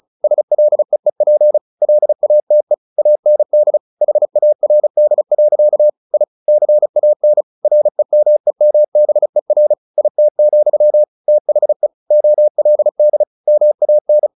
Morse code, heard as a keyed tone, about 35 wpm, 600 Hz. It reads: SLEEP LATE AND HARD. I CAN REMEMBER IT, THE OLD MAN
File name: the old man and the sea / the old man and the sea 1 35wpm